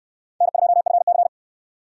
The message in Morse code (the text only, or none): N2FC